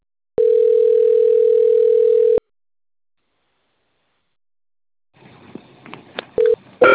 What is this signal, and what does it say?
Ringback tone (North America), 440 + 480 Hz.
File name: out- - - - - .31058